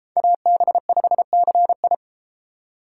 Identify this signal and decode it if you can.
Morse code, heard as a keyed tone, about 32 wpm, 700 Hz. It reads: AB5CI